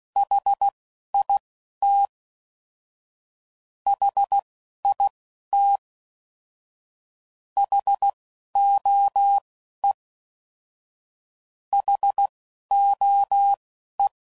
Morse code, heard as a keyed tone, about 16 wpm, spaced as for 11 wpm, 800 Hz. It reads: HIT HIT HOE HOE